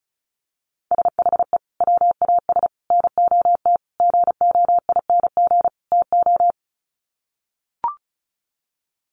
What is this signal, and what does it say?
Morse code, heard as a keyed tone, about 34 wpm, 700 Hz. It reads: SHE WAS NOT GOING TO
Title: Sets of 5 Words Encore2 35wpm